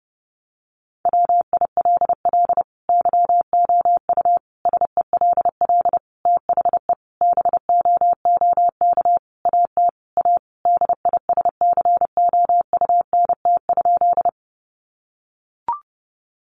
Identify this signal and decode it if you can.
Morse code, heard as a keyed tone, about 30 wpm, 700 Hz. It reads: WILL YOU SELL THE BOOK AT A DISCOUNT?
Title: Sentences from Top 1200 Words 30wpm